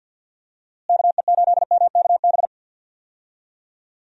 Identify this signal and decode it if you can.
Morse code, heard as a keyed tone, about 47 wpm, 700 Hz. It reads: KE8MKB
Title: USA Call Signs - 6 Characters - Repeated 2x - 50wpm